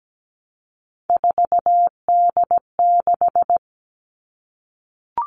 Morse code, heard as a keyed tone, about 17 wpm, 700 Hz. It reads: 4D6
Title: All Cut Numbers - 3-Digit 17wpm